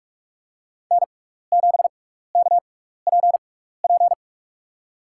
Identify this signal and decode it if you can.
Morse code, heard as a keyed tone, about 45 wpm, spaced as for 15 wpm, 700 Hz. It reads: N7KPP
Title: USA Call Signs - 5 Characters - 6x Character Spacing - 45wpm